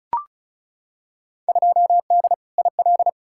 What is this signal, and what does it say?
Morse code, heard as a keyed tone, about 36 wpm, 700 Hz. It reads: 2D IL
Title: Field Day Exchanges - 35wpm